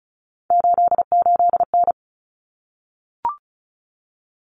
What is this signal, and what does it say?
Morse code, heard as a keyed tone, about 34 wpm, 700 Hz. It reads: 88N
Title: Common Cut Numbers - 3-Digit 35wpm